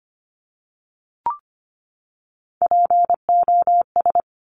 Morse code, heard as a keyed tone, about 25 wpm, 700 Hz. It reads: POS
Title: Top Three Letter Combinations in Words 25wpm